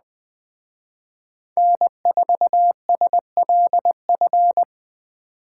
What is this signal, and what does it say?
Morse code, heard as a keyed tone, about 20 wpm, 700 Hz. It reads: N4SLF
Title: Speed Racing - USA 1x3 Call Signs 17wpm